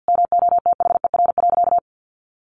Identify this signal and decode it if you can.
Morse code, heard as a keyed tone, about 50 wpm, 700 Hz. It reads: MOTHER.
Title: Sentences from Top 700 Words 50wpm